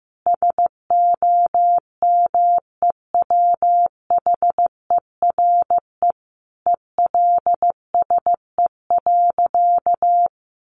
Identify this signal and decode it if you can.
Morse code, heard as a keyed tone, about 15 wpm, 700 Hz. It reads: SOMEWHERE ELSE.
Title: Sentences from Top 1400 Words 15wpm